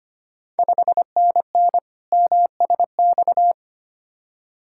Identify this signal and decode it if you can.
Morse code, heard as a keyed tone, about 25 wpm, 700 Hz. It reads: 5NN MSX
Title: Ontario QSO Party Exchange 25wpm